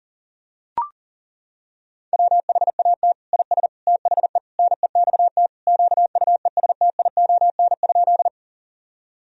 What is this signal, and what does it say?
Morse code, heard as a keyed tone, about 37 wpm, 700 Hz. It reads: WHAT IS THE NEXT QUESTION?
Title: Sentences from Top 1000 Words - Review Part 1 - 40wpm